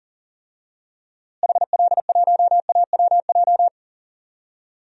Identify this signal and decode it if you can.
Morse code, heard as a keyed tone, about 40 wpm, 700 Hz. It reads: HL1AWJ
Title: DXCC-100 Call Signs - 6 Characters - Repeated 2x - 40wpm